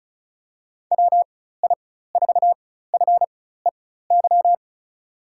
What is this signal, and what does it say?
Morse code, heard as a keyed tone, about 36 wpm, spaced as for 16 wpm, 700 Hz. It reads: WI4FEY
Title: USA Call Signs - 4x Character Spacing - 35wpm